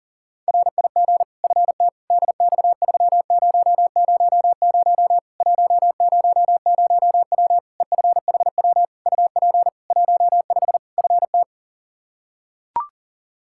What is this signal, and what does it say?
Morse code, heard as a keyed tone, about 40 wpm, 700 Hz. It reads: RIG FT DX3000 100W EFHW UP 15 FT
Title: Practice QSO - Exchange 2 - Info - 40wpm